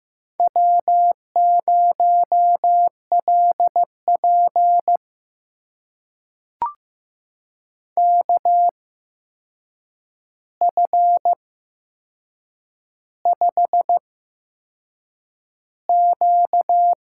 Morse code, heard as a keyed tone, about 15 wpm, spaced as for 6 wpm, 700 Hz. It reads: W0LP KF5Q
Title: USA Call Signs - 8x Character Spacing - 15wpm